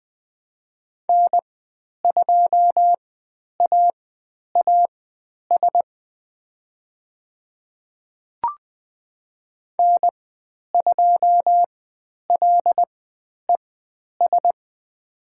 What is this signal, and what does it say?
Morse code, heard as a keyed tone, about 21 wpm, spaced as for 10 wpm, 700 Hz. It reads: N2AAS N2LES
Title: 102 Lesson 17g - Letter u - Call Signs 20wpm